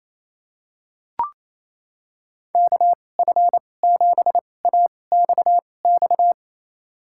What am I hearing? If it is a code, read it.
Morse code, heard as a keyed tone, about 28 wpm, spaced as for 20 wpm, 700 Hz. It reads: KF7AXX
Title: USA Call Signs - 6 Characters - 2x Character Spacing - 28wpm